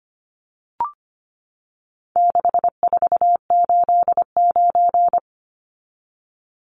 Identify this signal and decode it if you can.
Morse code, heard as a keyed tone, about 25 wpm, 700 Hz. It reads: 6489
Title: Numbers - 4-Digit 25wpm